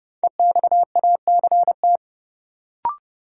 Morse code, heard as a keyed tone, about 30 wpm, 700 Hz. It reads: EXACT
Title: Sentences from Top 1300 Words - New Word Drill Alt - 30wpm